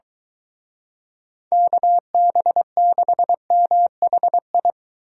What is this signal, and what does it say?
Morse code, heard as a keyed tone, about 23 wpm, 700 Hz. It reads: KB6MHI